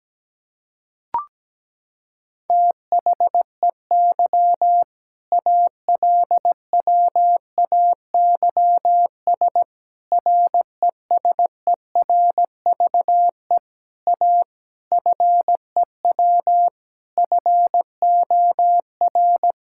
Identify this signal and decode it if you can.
Morse code, heard as a keyed tone, about 17 wpm, 700 Hz. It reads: THEY ALWAYS RESERVE A FEW FOR
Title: Sentences from Top 1000 Words - Review Part 6 - 17wpm